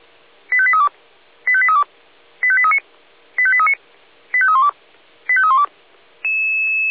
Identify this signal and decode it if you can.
Five-tone selective call (ZVEI), heard as quick runs of five tones, tones of about 70 ms. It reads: 85832 85832 85839 85839 85312 85312